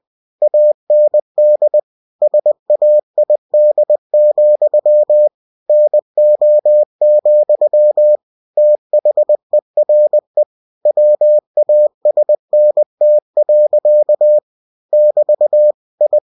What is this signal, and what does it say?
Morse code, heard as a keyed tone, about 20 wpm, 600 Hz. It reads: AND SAID, NO, THERE WASNT. <BT> I
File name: cw-books / winnie-the-pooh / winnie-20wpm-page-0016